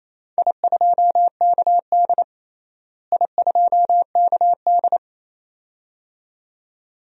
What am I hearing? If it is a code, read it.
Morse code, heard as a keyed tone, about 28 wpm, 700 Hz. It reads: I2KD I2KD
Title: DXCC-100 Call Signs - 4 Characters - Repeated 2x - 28wpm